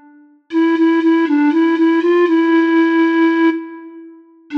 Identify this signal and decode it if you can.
There is no code.